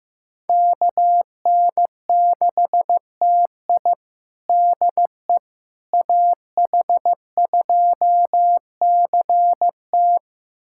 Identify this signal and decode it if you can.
Morse code, heard as a keyed tone, about 16 wpm, 700 Hz. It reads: KN6TI DE AH2CT
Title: USA Callsign DE Callsign - 5 Characters - 15wpm